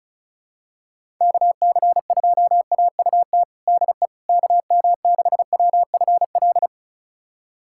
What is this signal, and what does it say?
Morse code, heard as a keyed tone, about 36 wpm, 700 Hz. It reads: KC2AUT DE KM6WFL
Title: USA Callsign DE Callsign - 6 Characters - 35wpm